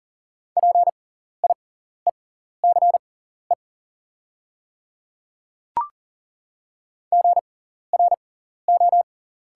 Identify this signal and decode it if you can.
Morse code, heard as a keyed tone, about 38 wpm, spaced as for 14 wpm, 700 Hz. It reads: PIECE GRO
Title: 6x Character Spacing - Top 500 Words - 40wpm